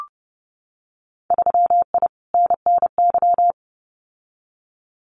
Morse code, heard as a keyed tone, about 30 wpm, 700 Hz. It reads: 3I NNY
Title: Winter Field Day Exchanges 30wpm